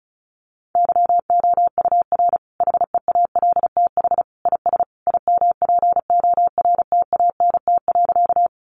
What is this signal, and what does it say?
Morse code, heard as a keyed tone, about 34 wpm, 700 Hz. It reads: YOUR HEALTH IS IMPORTANT.
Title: Sentences from Top 600 Words 35wpm